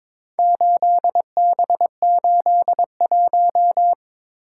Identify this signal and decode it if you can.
Morse code, heard as a keyed tone, about 22 wpm, 700 Hz. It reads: 8B81
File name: practice / 22wpm / Four Letter-Number Combinations - 22wpm